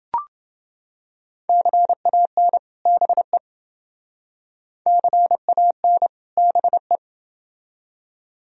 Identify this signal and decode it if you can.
Morse code, heard as a keyed tone, about 28 wpm, 700 Hz. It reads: CAN BE CAN BE